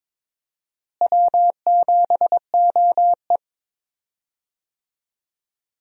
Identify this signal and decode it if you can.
Morse code, heard as a keyed tone, about 21 wpm, 700 Hz. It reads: W7OE
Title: USA Call Signs - 4 Characters - Repeated 3x - 22wpm